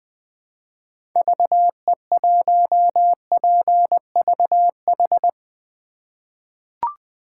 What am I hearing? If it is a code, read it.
Morse code, heard as a keyed tone, about 20 wpm, 700 Hz. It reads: VE1PVH
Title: 2023 SOTA Chaser Call Signs - 20wpm